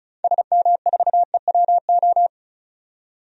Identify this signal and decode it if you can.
Morse code, heard as a keyed tone, about 36 wpm, 700 Hz. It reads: SM4EWO
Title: DXCC-100 Call Signs - 6 Characters - 35wpm